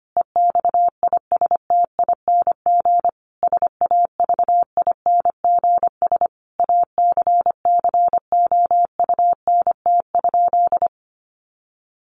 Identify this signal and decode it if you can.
Morse code, heard as a keyed tone, about 25 wpm, 700 Hz. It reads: EXISTING SAVINGS ACCOUNT?
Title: Sentences from Top 1500 Words 25wpm